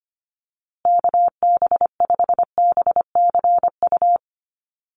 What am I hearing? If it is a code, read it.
Morse code, heard as a keyed tone, about 25 wpm, 700 Hz. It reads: KB5BCU
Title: USA Call Signs Repeated 2x 25wpm